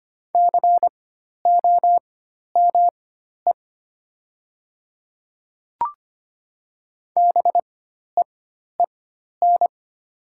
Morse code, heard as a keyed tone, about 25 wpm, spaced as for 12 wpm, 700 Hz. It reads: COME BEEN
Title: 4x Character Spacing - Top 100 Words - 25wpm